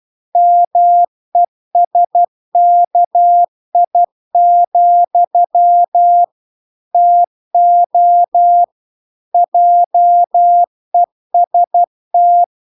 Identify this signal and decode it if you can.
Morse code, heard as a keyed tone, about 12 wpm, 700 Hz. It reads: MESKI, TO JEST